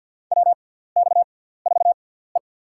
Morse code, heard as a keyed tone, about 52 wpm, spaced as for 17 wpm, 700 Hz. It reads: WX4E
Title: USA Call Signs - 4 Characters - 6x Character Spacing - 50wpm